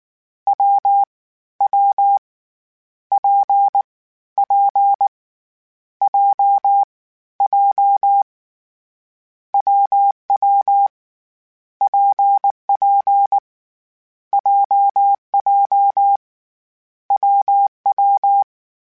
Morse code, heard as a keyed tone, about 19 wpm, 800 Hz. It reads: W W P P J J WW PP JJ WW
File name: Variable Speed / Variable Speed-mp3s / WPJ WPJ1 sp @18